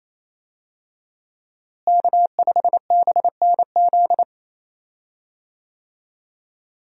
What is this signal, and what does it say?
Morse code, heard as a keyed tone, about 28 wpm, 700 Hz. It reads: K5BNZ
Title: USA Call Signs - 5 Characters - Repeated 2x - 28wpm